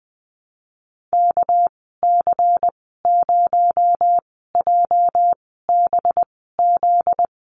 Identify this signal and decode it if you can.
Morse code, heard as a keyed tone, about 20 wpm, spaced as for 14 wpm, 700 Hz. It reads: KC0JBZ
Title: 245 Lesson 39g - Letter z - Call Signs 20wpm